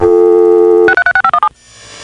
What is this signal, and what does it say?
DTMF, heard as telephone touch tones, tones of about 80 ms. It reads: 636517